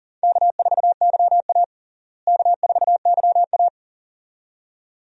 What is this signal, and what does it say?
Morse code, heard as a keyed tone, about 40 wpm, 700 Hz. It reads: K4YA K4YA
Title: USA Call Signs - 4 Characters - Repeated 2x - 40wpm